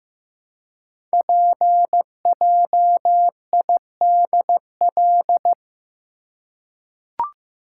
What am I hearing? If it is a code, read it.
Morse code, heard as a keyed tone, about 15 wpm, 700 Hz. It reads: PJIDL
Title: Code Groups - 5 Letters - 15wpm